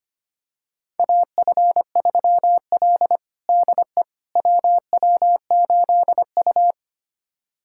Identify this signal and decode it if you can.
Morse code, heard as a keyed tone, about 25 wpm, 700 Hz. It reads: AF3L DE WW8U
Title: USA Callsign DE Callsign - 4 Characters - 25wpm